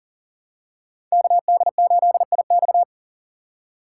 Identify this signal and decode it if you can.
Morse code, heard as a keyed tone, about 40 wpm, 700 Hz. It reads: KD8IX
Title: USA Call Signs - 5 Characters - 6x Character Spacing - 40wpm